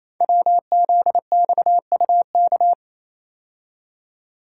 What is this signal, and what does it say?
Morse code, heard as a keyed tone, about 27 wpm, 700 Hz. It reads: WZXUK
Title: Code Groups - 5 Letter:Numbers - 28wpm